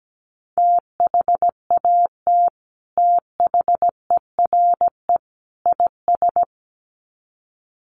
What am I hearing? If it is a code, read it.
Morse code, heard as a keyed tone, about 17 wpm, 700 Hz. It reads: THAT THERE IS
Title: Sets of 3 Words 17wpm